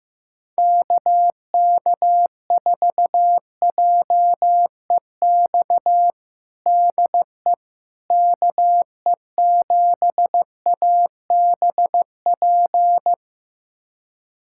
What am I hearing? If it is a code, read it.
Morse code, heard as a keyed tone, about 15 wpm, 700 Hz. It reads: KK4JEX DE KE7ABP
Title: USA Callsign DE Callsign - 6 Characters - 15wpm